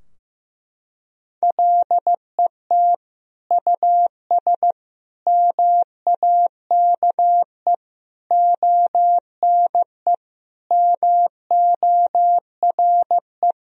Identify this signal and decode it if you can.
Morse code, heard as a keyed tone, about 15 wpm, 700 Hz. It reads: LET US MAKE ONE MORE
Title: Sentences from Top 1000 Words - Review Part 4 - 15wpm